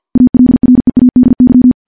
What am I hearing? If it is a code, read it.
Morse code, heard as a keyed tone, about 50 wpm, 250 Hz. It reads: ALREADY
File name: morse-training-50WPM